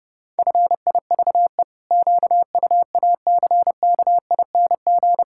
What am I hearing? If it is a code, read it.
Morse code, heard as a keyed tone, about 32 wpm, 700 Hz. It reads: FIVE QUACKING
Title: Pangram Challenge v2 - 30wpm